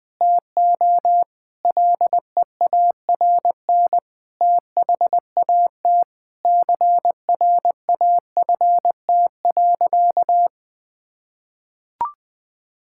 Morse code, heard as a keyed tone, about 20 wpm, 700 Hz. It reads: TO LEARN THAT CRAFT.
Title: Sentences from Top 1000 Words - Review Part 6 - 20wpm